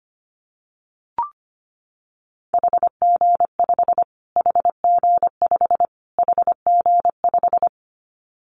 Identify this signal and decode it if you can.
Morse code, heard as a keyed tone, about 25 wpm, 700 Hz. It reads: HG5 HG5 HG5
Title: Callsign Prefixes for Top 25 DXCC - Repeated 3x - 25wpm